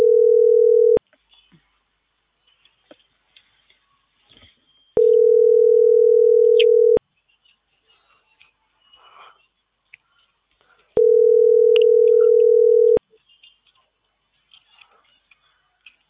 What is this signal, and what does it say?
Ringback tone (North America), 440 + 480 Hz.